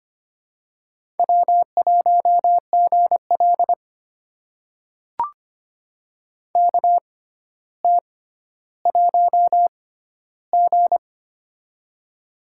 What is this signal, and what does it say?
Morse code, heard as a keyed tone, about 25 wpm, spaced as for 13 wpm, 700 Hz. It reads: W1GL KT1G
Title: USA Call Signs - 4 Characters - 6x Character Spacing - 25wpm